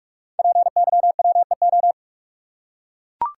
Morse code, heard as a keyed tone, about 44 wpm, 700 Hz. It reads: PYWEO
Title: Code Groups - 5 Letters - 45wpm